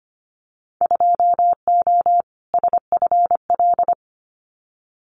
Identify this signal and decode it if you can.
Morse code, heard as a keyed tone, about 25 wpm, 700 Hz. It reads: 2O SFL
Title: Winter Field Day Exchanges 25wpm